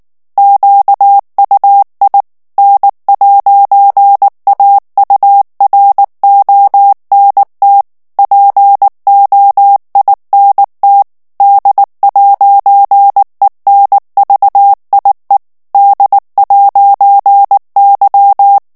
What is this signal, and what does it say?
Morse code, heard as a keyed tone, about 19 wpm, 800 Hz. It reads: QUI N'AURONT POINT D'ENVIE D'Y
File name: FRANCAIS TEXTE CLAIR 2 VITESSE 18 MOTS MINUTES